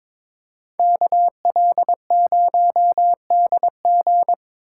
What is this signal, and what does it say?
Morse code, heard as a keyed tone, about 25 wpm, 700 Hz. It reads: KL0DG
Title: Speed Racing - USA 2x2 Call Signs 22wpm